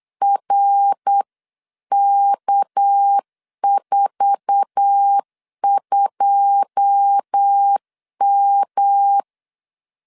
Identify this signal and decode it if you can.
Morse code, heard as a keyed tone, about 8 wpm, 800 Hz. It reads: RK42M
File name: LEZIONI / 10 SIGNS / 10A SLASH 8wpm 3452